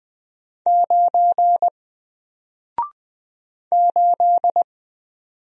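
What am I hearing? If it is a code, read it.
Morse code, heard as a keyed tone, about 20 wpm, 700 Hz. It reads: 9 8
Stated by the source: Single letter and number Morse drill at 20 wpm, from N4LCD.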